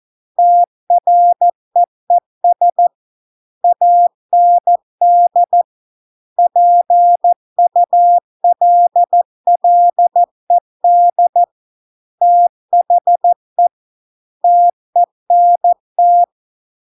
Morse code, heard as a keyed tone, about 14 wpm, 700 Hz. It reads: TREES AND PULLED THE TENT